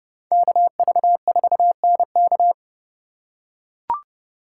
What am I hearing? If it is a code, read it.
Morse code, heard as a keyed tone, about 31 wpm, 700 Hz. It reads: KV4NK
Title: USA Call Signs - 5 Characters - 3x Character Spacing - 30wpm